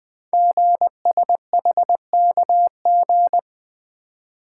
Morse code, heard as a keyed tone, about 20 wpm, 700 Hz. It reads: GSHKG
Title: Code Groups - 5 Letters - 20wpm